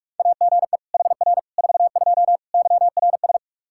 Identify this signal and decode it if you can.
Morse code, heard as a keyed tone, about 46 wpm, 700 Hz. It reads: AGE HR 42 YRS
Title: Practice QSO - Exchange 3 - More Info - 45wpm